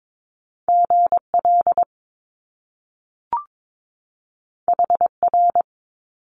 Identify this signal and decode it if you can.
Morse code, heard as a keyed tone, about 22 wpm, 700 Hz. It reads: GL HR
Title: Top Two Letter Combinations in Words 22wpm